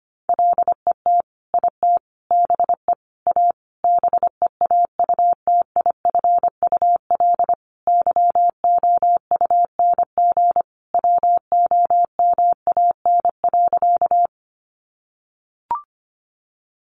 Morse code, heard as a keyed tone, about 25 wpm, 700 Hz. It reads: LET IT BE A BEAUTIFUL YOUNG WOMAN.
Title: Sentences from Top 1100 Words 25wpm